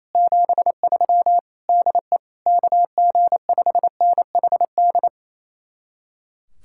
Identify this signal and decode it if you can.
Morse code, heard as a keyed tone, about 34 wpm, 700 Hz. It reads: 73 DE KG5NHD
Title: Speed-Racing Common QSO Phrases - 28wpm